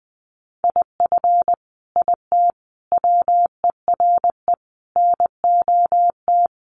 Morse code, heard as a keyed tone, about 19 wpm, 700 Hz. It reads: IF IT WERE NOT
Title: Sets of 5 Words 20wpm